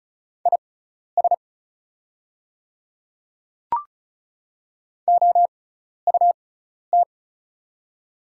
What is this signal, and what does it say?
Morse code, heard as a keyed tone, about 36 wpm, spaced as for 12 wpm, 700 Hz. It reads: IS OUT
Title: 6x Character Spacing - Top 100 Words - 35wpm